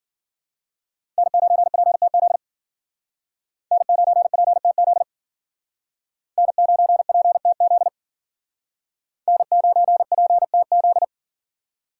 Morse code, heard as a keyed tone, about 53 wpm, 700 Hz. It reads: N9PTZ N9PTZ N9PTZ N9PTZ